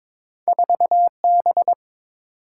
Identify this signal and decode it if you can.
Morse code, heard as a keyed tone, about 22 wpm, 700 Hz. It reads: 4B
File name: practice / 22wpm / Two Letter-Number Combinations - 22wpm